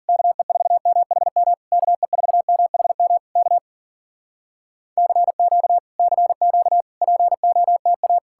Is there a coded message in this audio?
Yes: Morse code from 0.1 to 8.2 s, heard as a keyed tone, about 53 wpm, 700 Hz. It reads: KE4MHM KE4MHM K CQ CQ POTA